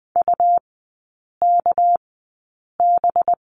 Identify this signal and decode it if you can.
Morse code, heard as a keyed tone, about 20 wpm, spaced as for 8 wpm, 690 Hz. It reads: UKB
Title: 198 Lesson 32e - Letter k - Three Characters 20wpm